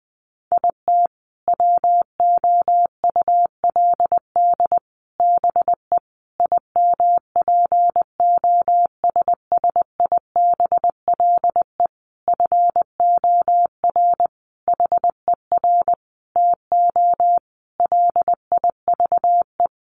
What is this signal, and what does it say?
Morse code, heard as a keyed tone, about 20 wpm, 700 Hz. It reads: IT WOULD BE IMPOSSIBLE FOR HER TO LIVE
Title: Sentences from Top 1100 Words 20wpm